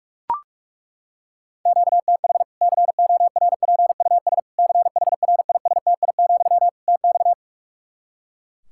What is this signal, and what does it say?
Morse code, heard as a keyed tone, about 42 wpm, 700 Hz. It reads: QTH CORPUS CHRISTI, TX
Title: K1USN SST QSO Practice 45wpm